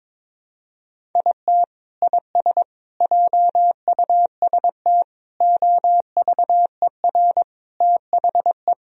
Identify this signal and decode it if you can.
Morse code, heard as a keyed tone, about 22 wpm, 700 Hz. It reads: IT IS JUST OVER THE
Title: Sentences from Top 1000 Words - Review Part 1 - 22wpm